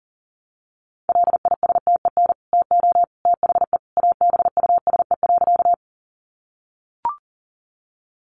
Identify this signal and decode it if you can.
Morse code, heard as a keyed tone, about 40 wpm, 700 Hz. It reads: LISTEN TO THE ABUSE.
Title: Sentences from Top 1300 Words 40wpm